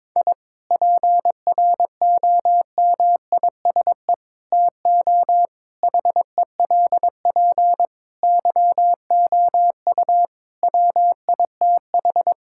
Morse code, heard as a keyed tone, about 22 wpm, 700 Hz. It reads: I PROMISE TO HELP YOU WITH
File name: Difficult Sentences with Hints - 22wpm